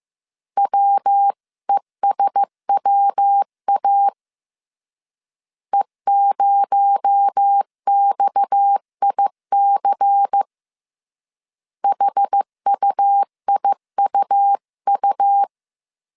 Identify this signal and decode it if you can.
Morse code, heard as a keyed tone, about 15 wpm, 800 Hz. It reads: WESWA E0XIC HUIUU